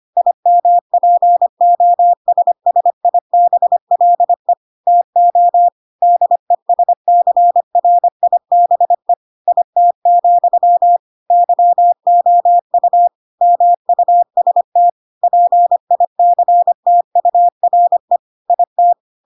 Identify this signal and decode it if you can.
Morse code, heard as a keyed tone, about 25 wpm, 700 Hz. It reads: IMPOSSIBLE TO DESCRIBE IT, YOU MUST PICTURE IT
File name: Pinocchio / Pinocchio ch5 25 25 ik1hge